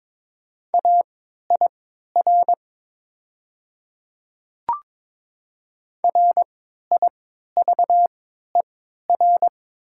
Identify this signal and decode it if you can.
Morse code, heard as a keyed tone, about 22 wpm, spaced as for 13 wpm, 700 Hz. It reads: AIR RIVER
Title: 3x Character Spacing - Top 300 Words - 22wpm